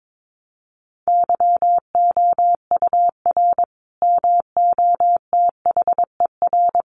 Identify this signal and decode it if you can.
Morse code, heard as a keyed tone, about 22 wpm, 700 Hz. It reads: YOUR MOTHER
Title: Sentences from Top 1200 Words 22wpm